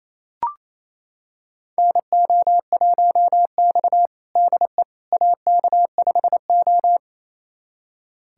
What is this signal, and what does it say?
Morse code, heard as a keyed tone, about 28 wpm, 690 Hz. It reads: NO1X DE AK5O
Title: USA Callsign DE Callsign - 4 Characters - 28wpm